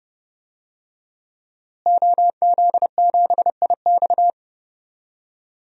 Morse code, heard as a keyed tone, about 30 wpm, 700 Hz. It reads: OZ7IX
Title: DXCC-100 Call Signs - 5 Characters - Repeated 3x - 30wpm